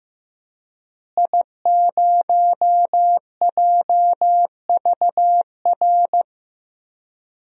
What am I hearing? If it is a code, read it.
Morse code, heard as a keyed tone, about 15 wpm, 700 Hz. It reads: I0JVR
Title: DXCC-100 Call Signs - 5 Characters - Repeated 3x - 15wpm